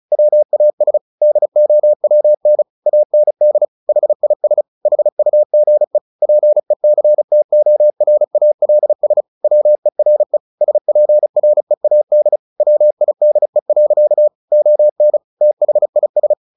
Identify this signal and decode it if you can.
Morse code, heard as a keyed tone, about 34 wpm, 600 Hz. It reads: WAS DOWN AND HIS HUGE PECTORALS WERE SPREAD WIDE. ON THIS
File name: the old man and the sea 2 35wpm